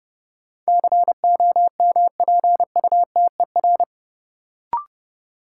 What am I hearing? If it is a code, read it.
Morse code, heard as a keyed tone, about 30 wpm, 700 Hz. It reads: COMPUTER
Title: Sentences from Top 1000 Words - Review Part 3 - New Word Drill - 30wpm